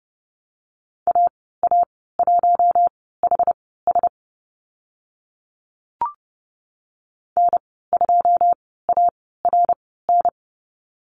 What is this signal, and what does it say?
Morse code, heard as a keyed tone, about 30 wpm, spaced as for 17 wpm, 700 Hz. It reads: AA1HS N2ARN
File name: Lesson 15g - Number 5 - Call Signs 30wpm